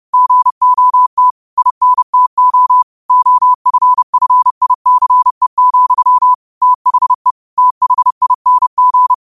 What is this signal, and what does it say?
Morse code, heard as a keyed tone, about 30 wpm, 1000 Hz. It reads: GOT INTO OFFICE, THE THING